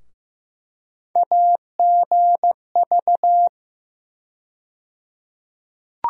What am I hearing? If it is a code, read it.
Morse code, heard as a keyed tone, about 15 wpm, 700 Hz. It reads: AGV
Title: USA Call Sign Postfixes - 3 Letters 15wpm